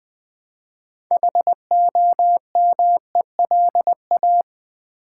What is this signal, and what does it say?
Morse code, heard as a keyed tone, about 20 wpm, 700 Hz. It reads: HOMELA
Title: Build A Word - 3 Letters At A Time - 8-Letter Words - 20wpm